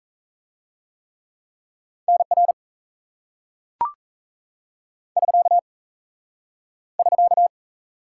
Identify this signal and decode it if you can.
Morse code, heard as a keyed tone, about 38 wpm, 700 Hz. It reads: NR <SK> <SK>